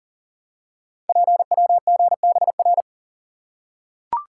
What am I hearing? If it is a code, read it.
Morse code, heard as a keyed tone, about 38 wpm, 700 Hz. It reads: PWGBR
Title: Code Groups - 5 Letters - 40wpm